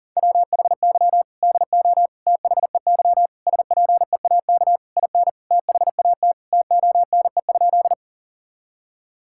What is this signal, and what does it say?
Morse code, heard as a keyed tone, about 40 wpm, 700 Hz. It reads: WHY DO THEY SPEAK IN THAT TONE?
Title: Sentences from Top 1600 Words 40wpm